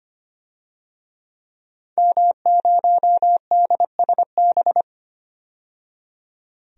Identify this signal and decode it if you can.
Morse code, heard as a keyed tone, about 25 wpm, 700 Hz. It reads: M0DSB
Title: DXCC-100 Call Signs - 5 Characters - 25wpm